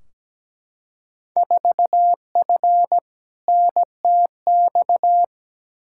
Morse code, heard as a keyed tone, about 17 wpm, 700 Hz. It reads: 4F NTX
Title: Field Day Exchanges Encore - 17wpm